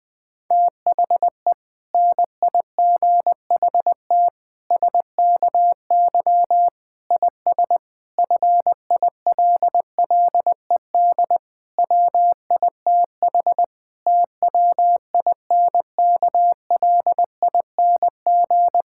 Morse code, heard as a keyed tone, about 20 wpm, 700 Hz. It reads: THE NIGHT SKY IS FILLED WITH TWINKLING
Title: Difficult Sentences with Hints - 20wpm